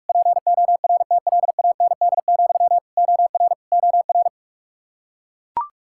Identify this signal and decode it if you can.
Morse code, heard as a keyed tone, about 41 wpm, 700 Hz. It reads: PORTLAND, OR OR
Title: Practice QSO - Exchange 1 - Beginning - 45wpm